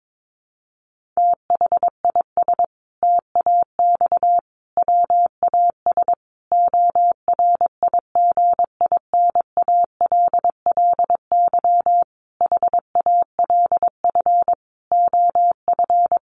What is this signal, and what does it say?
Morse code, heard as a keyed tone, about 22 wpm, 700 Hz. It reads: THIS TAX WAS ORIGINALLY HALF OF
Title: Sentences from Top 1500 Words 22wpm